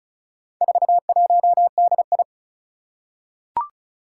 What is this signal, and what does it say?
Morse code, heard as a keyed tone, about 36 wpm, 700 Hz. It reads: 41DI
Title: Four Challenging Letter-Number Combinations 35wpm